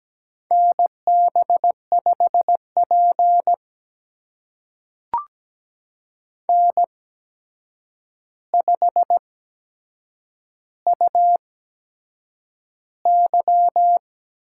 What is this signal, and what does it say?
Morse code, heard as a keyed tone, about 17 wpm, spaced as for 7 wpm, 700 Hz. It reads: NB5P N5UY
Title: USA Call Signs - 4 Characters - 8x Character Spacing - 17wpm